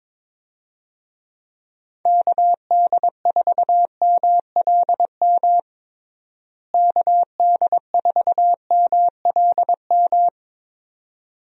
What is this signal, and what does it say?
Morse code, heard as a keyed tone, about 21 wpm, 700 Hz. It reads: KD4MLM KD4MLM
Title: USA Call Signs - 6 Characters - Repeated 2x - 22wpm